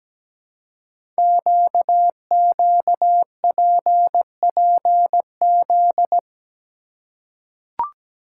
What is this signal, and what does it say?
Morse code, heard as a keyed tone, about 17 wpm, 700 Hz. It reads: QQPPZ
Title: Code Groups - 5 Letters - 17wpm